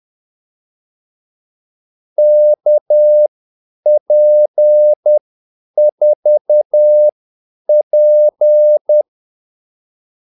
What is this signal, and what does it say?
Morse code, heard as a keyed tone, about 10 wpm, spaced as for 8 wpm, 600 Hz. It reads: KP4P
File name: Callsigns-10-8wpm